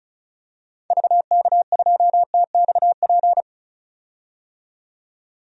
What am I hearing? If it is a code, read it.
Morse code, heard as a keyed tone, about 35 wpm, 700 Hz. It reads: VK2TXP
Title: DXCC-100 Call Signs - 6 Characters - 35wpm